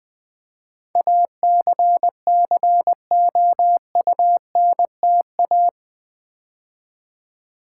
Morse code, heard as a keyed tone, about 20 wpm, 700 Hz. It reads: ACCOUNTA
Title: Build A Word - 4 Letters At A Time - 10-Letter Words - 20wpm